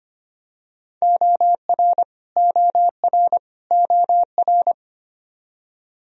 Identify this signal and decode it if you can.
Morse code, heard as a keyed tone, about 25 wpm, 700 Hz. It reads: OR OR OR
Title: Top 100 Words Repeated 25wpm